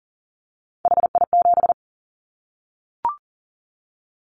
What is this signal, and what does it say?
Morse code, heard as a keyed tone, about 40 wpm, 700 Hz. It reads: HI7
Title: Callsign Prefixes for Top 100 DXCC - 40wpm